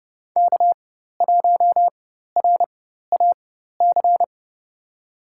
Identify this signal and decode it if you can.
Morse code, heard as a keyed tone, about 29 wpm, spaced as for 14 wpm, 700 Hz. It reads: K1RAC
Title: USA Call Signs - 5 Characters - 4x Character Spacing - 30wpm